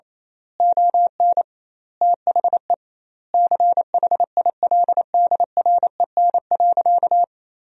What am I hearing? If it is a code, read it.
Morse code, heard as a keyed tone, about 28 wpm, 700 Hz. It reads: ON THE CHILDREN.
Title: 2x Word-Spacing - Sentences from Top 300 Words - 28wpm